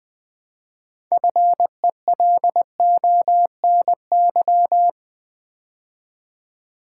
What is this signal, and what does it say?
Morse code, heard as a keyed tone, about 20 wpm, 700 Hz. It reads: FELONY
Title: Build A Word - 3 Letters At A Time - 6-Letter Words - 20wpm